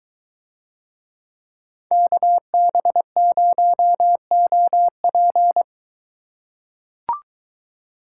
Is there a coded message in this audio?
Yes: Morse code from 1.9 to 5.6 s, heard as a keyed tone, about 21 wpm, 700 Hz. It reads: KB0OP